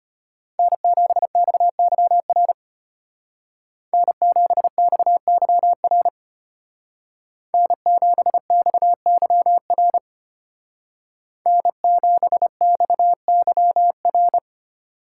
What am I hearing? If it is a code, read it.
Morse code, heard as a keyed tone, about 33 wpm, 700 Hz. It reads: N7XYR N7XYR N7XYR N7XYR